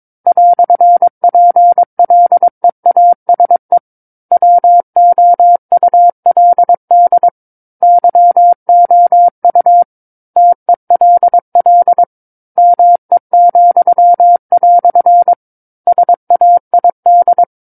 Morse code, heard as a keyed tone, about 22 wpm, 700 Hz. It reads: "PLEASE WOULD YOU TELL ME," SAID
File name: 06-Chapter-06 22 WPM